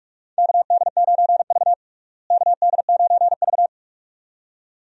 Morse code, heard as a keyed tone, about 41 wpm, 700 Hz. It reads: KD9V KD9V
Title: USA Call Signs - 4 Characters - Repeated 2x - 45wpm